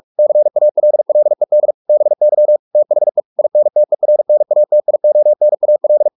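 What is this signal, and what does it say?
Morse code, heard as a keyed tone, about 44 wpm, 600 Hz. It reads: CALLED BY THE INTERNATIONAL
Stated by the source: Morse practice at 45 wpm, the novel Never by Ken Follett in English, from CW med Gnister.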